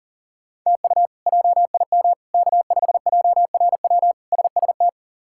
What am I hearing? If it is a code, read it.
Morse code, heard as a keyed tone, about 40 wpm, 700 Hz. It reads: TU JIM K5JRW SST
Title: K1USN SST QSO Practice - 40wpm